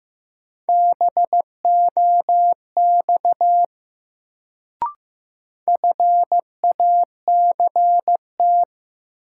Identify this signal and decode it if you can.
Morse code, heard as a keyed tone, about 15 wpm, 700 Hz. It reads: BOX FACT
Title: Sentences from Top 500 Words - New Word Drill - 15wpm